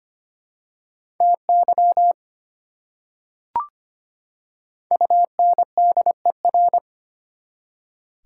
Morse code, heard as a keyed tone, about 25 wpm, 700 Hz. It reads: TY UNDER
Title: Prefixes and Suffixes 25wpm